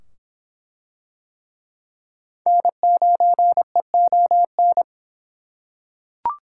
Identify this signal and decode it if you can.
Morse code, heard as a keyed tone, about 24 wpm, 700 Hz. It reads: N9EON